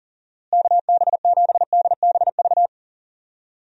Morse code, heard as a keyed tone, about 40 wpm, 700 Hz. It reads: KB7DBV
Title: USA Call Signs - 6 Characters - Repeated 2x - 40wpm